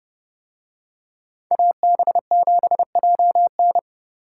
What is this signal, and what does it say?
Morse code, heard as a keyed tone, about 30 wpm, 700 Hz. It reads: AB7JN